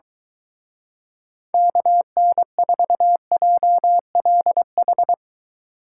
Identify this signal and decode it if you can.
Morse code, heard as a keyed tone, about 23 wpm, 700 Hz. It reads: KN4JLH